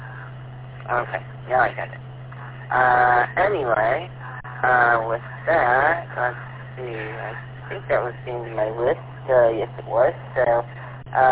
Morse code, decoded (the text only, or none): none